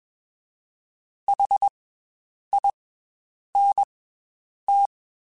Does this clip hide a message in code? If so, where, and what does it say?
Yes: Morse code from 1.3 to 4.9 s, heard as a keyed tone, about 21 wpm, 800 Hz. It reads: H I N T